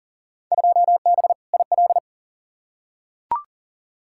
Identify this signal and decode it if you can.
Morse code, heard as a keyed tone, about 40 wpm, 700 Hz. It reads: 2B IL
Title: Field Day Exchanges Encore - 40wpm